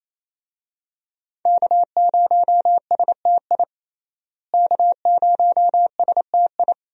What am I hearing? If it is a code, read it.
Morse code, heard as a keyed tone, about 28 wpm, 700 Hz. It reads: K0STI K0STI
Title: USA Call Signs - 5 Characters - Repeated 2x - 28wpm